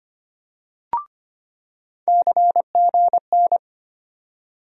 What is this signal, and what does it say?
Morse code, heard as a keyed tone, about 25 wpm, 700 Hz. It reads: CGN
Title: USA Call Sign Postfixes - 3 Letters 25wpm